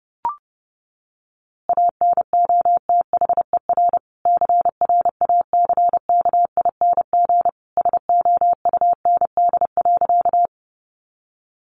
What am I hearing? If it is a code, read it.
Morse code, heard as a keyed tone, about 28 wpm, 700 Hz. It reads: ANOTHER CRACKING SOUND.
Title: A Sound Of Thunder 30wpm